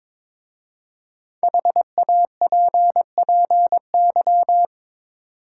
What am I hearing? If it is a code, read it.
Morse code, heard as a keyed tone, about 22 wpm, 690 Hz. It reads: HAPPY
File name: 5 Letter Words - First 3 Letters Given - 22wpm